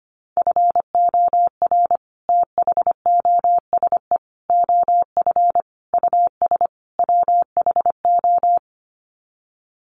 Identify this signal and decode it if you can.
Morse code, heard as a keyed tone, about 25 wpm, 700 Hz. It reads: FOR THOSE OF US WHO
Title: Sets of 5 Words Encore 25wpm